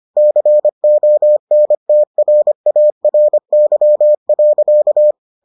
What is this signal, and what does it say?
Morse code, heard as a keyed tone, about 25 wpm, 600 Hz. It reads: CONTRARY.